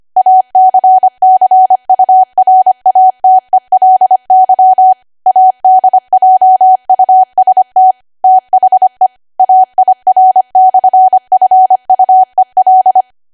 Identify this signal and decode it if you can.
Morse code, heard as a keyed tone, about 25 wpm, 750 Hz. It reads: ACCURATELY ADJUST THE AIR/FUEL
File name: Archive / 25 WPM / 25WPM